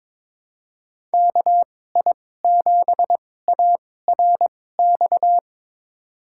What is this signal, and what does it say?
Morse code, heard as a keyed tone, about 21 wpm, spaced as for 16 wpm, 700 Hz. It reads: KI7ARX
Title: USA Call Signs - 6 Characters - 2x Character Spacing - 22wpm